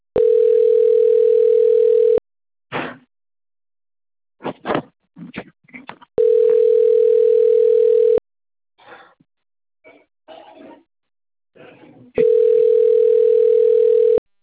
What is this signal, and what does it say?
Ringback tone (North America), 440 + 480 Hz.